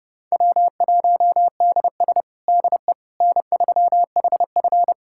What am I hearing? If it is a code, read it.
Morse code, heard as a keyed tone, about 30 wpm, 700 Hz. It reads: W1DS DE N3HF
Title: USA Callsign DE Callsign - 4 Characters - 30wpm